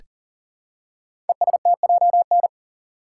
Morse code, heard as a keyed tone, about 42 wpm, 700 Hz. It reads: ESTJN
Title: Code Groups - 5 Letter:Numbers - 40wpm